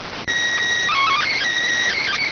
SSTV (slow-scan television), PD 290.